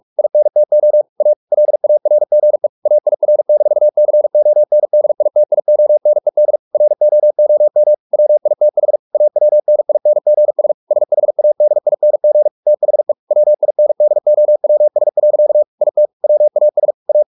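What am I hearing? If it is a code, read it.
Morse code, heard as a keyed tone, about 44 wpm, 600 Hz. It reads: INTO A LARGE AIR-CONDITIONED ROOM WITH AWNINGS SHADING THE WINDOWS. IT WAS A
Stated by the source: Morse practice at 45 wpm, the novel Never by Ken Follett in English, from CW med Gnister.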